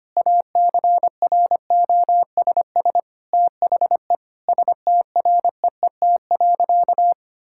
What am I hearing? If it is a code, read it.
Morse code, heard as a keyed tone, about 25 wpm, 700 Hz. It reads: ACROSS THE STREET.
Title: Sentences from Top 1000 Words - Review Part 5 - 25wpm